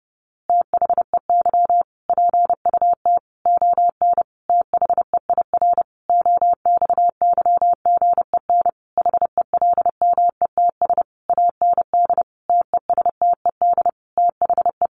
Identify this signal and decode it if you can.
Morse code, heard as a keyed tone, about 30 wpm, 700 Hz. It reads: THEY PUT ON THEIR OXYGEN HELMETS AND TESTED THE
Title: A Sound Of Thunder 30wpm